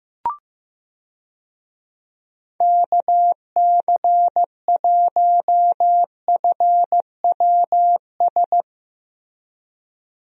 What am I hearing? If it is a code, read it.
Morse code, heard as a keyed tone, about 15 wpm, 700 Hz. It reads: KC1FWS
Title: USA Call Signs - 6 Characters - Repeated 2x - 15wpm